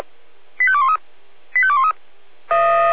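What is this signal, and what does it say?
Five-tone selective call (ZVEI), heard as quick runs of five tones, tones of about 70 ms. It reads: 85213 85213
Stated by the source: ZVEI five-tone selective call, received by SDR.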